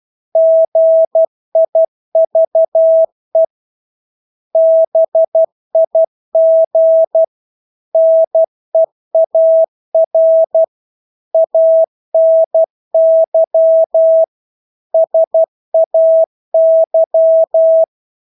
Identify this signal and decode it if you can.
Morse code, heard as a keyed tone, about 12 wpm, 650 Hz. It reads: GIVE BIG NEAR ANY SAY
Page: Engelske ord 12 wpm